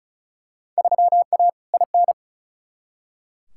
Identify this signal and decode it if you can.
Morse code, heard as a keyed tone, about 36 wpm, 700 Hz. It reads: 3A IN
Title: Field Day Exchanges Encore - 35wpm